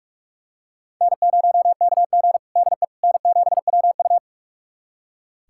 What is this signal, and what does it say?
Morse code, heard as a keyed tone, about 46 wpm, 700 Hz. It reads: N0KG DE N7WU
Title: USA Callsign DE Callsign - 4 Characters - 45wpm